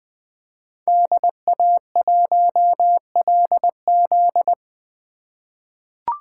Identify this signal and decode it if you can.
Morse code, heard as a keyed tone, about 20 wpm, 700 Hz. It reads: DA1LZ
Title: USA Call Signs - 5 Characters - 3x Character Spacing - 20wpm